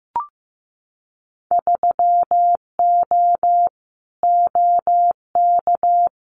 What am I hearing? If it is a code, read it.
Morse code, heard as a keyed tone, about 15 wpm, 700 Hz. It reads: 3O OK
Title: Winter Field Day Exchanges 15wpm